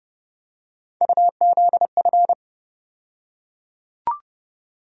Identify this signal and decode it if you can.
Morse code, heard as a keyed tone, about 30 wpm, 700 Hz. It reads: UZF
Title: USA Call Sign Postfixes - 3 Letters 30wpm